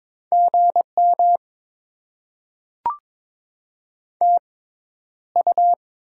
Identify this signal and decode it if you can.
Morse code, heard as a keyed tone, about 22 wpm, spaced as for 11 wpm, 700 Hz. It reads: GM TU
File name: Top 20 QSO Elements - 6x Character Spacing - 22wpm